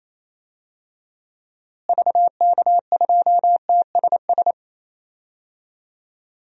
Morse code, heard as a keyed tone, about 27 wpm, 700 Hz. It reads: VK2TSS
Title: DXCC-100 Call Signs - 6 Characters - Repeated 2x - 28wpm